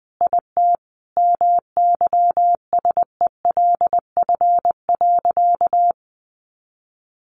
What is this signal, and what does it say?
Morse code, heard as a keyed tone, about 20 wpm, 700 Hz. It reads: IT MYSELF.
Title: Sentences from Top 1300 Words 20wpm